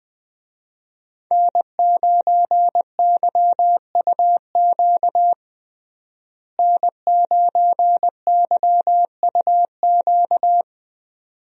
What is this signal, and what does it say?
Morse code, heard as a keyed tone, about 20 wpm, 700 Hz. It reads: N9YUQ N9YUQ
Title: USA Call Signs - 5 Characters - Repeated 2x - 20wpm